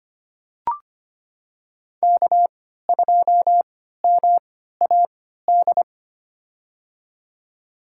Morse code, heard as a keyed tone, about 25 wpm, spaced as for 14 wpm, 700 Hz. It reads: K2MAD
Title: USA Call Signs - 3x Character Spacing - 25wpm